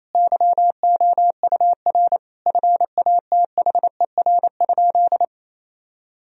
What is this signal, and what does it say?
Morse code, heard as a keyed tone, about 28 wpm, 700 Hz. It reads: YOUR FATHER?
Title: Sentences from Top 1000 Words - Review Part 3 - 28wpm